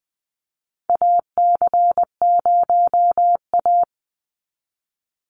Morse code, heard as a keyed tone, about 20 wpm, 700 Hz. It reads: AC0A
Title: NA SOTA Chaser Callsigns 20wpm